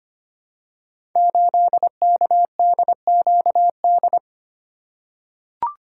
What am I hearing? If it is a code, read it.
Morse code, heard as a keyed tone, about 25 wpm, 700 Hz. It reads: 8KDQD
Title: Code Groups - 5 Letter:Numbers - 25wpm